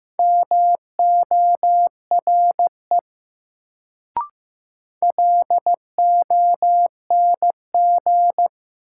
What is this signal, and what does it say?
Morse code, heard as a keyed tone, about 15 wpm, 700 Hz. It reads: MORE LONG
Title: Sets of 2 Words - New Word Drill - 15wpm